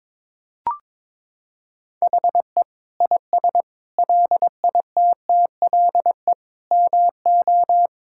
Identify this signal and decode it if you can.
Morse code, heard as a keyed tone, about 22 wpm, 700 Hz. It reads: HE IS LITTLE MO
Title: Sentences from Top 1600 Words 22wpm